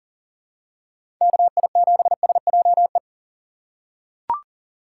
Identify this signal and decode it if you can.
Morse code, heard as a keyed tone, about 37 wpm, 700 Hz. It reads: KI7SJE
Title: USA Call Signs - 6x Character Spacing - 40wpm